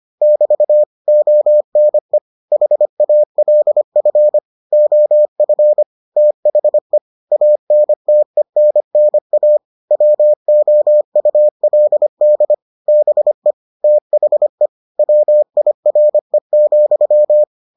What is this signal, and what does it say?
Morse code, heard as a keyed tone, about 25 wpm, 600 Hz. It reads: <BT> ONE HALF OF THE ANTENNA WOULD BE THE WIRE,